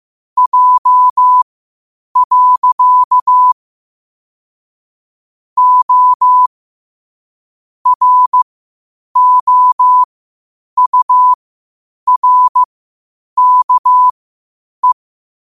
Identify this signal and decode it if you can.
Morse code, heard as a keyed tone, about 15 wpm, spaced as for 8 wpm, 1000 Hz. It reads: J. O ROURKE
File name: qotdfiles / 8wpm